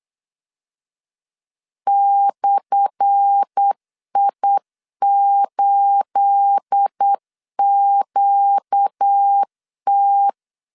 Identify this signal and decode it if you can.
Morse code, heard as a keyed tone, about 8 wpm, 800 Hz. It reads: /I8QT
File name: LEZIONI / 10 SIGNS / 10A SLASH 8wpm 3452